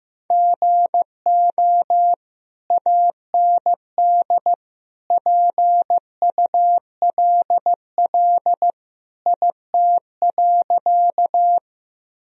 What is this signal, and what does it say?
Morse code, heard as a keyed tone, about 16 wpm, 700 Hz. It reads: GO AND PULL IT.
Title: Sentences from Top 1000 Words - Review Part 2 - 15wpm